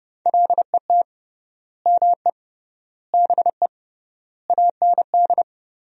Morse code, heard as a keyed tone, about 30 wpm, 700 Hz. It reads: LET ME BE AND
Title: 3x Word-Spacing - Sentences from Top 300 Words - 30wpm